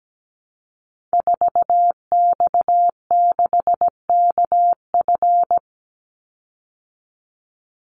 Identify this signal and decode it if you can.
Morse code, heard as a keyed tone, about 17 wpm, 700 Hz. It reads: 4X6KF
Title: Call Signs Repeated 3x - 17wpm